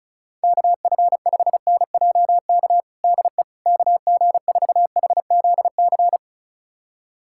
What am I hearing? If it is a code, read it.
Morse code, heard as a keyed tone, about 34 wpm, 690 Hz. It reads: KF5NJK DE KG4HZC
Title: USA Callsign DE Callsign - 6 Characters - 35wpm